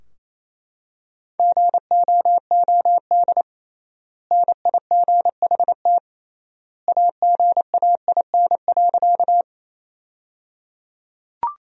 Morse code, heard as a keyed tone, about 28 wpm, 700 Hz. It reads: GOOD NIGHT AGAIN.
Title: 3x Word-Spacing - Sentences from Top 300 Words - 28wpm